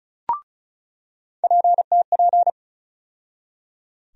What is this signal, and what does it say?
Morse code, heard as a keyed tone, about 36 wpm, 700 Hz. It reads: PTP
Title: USA Call Sign Postfixes - 3 Letters 35wpm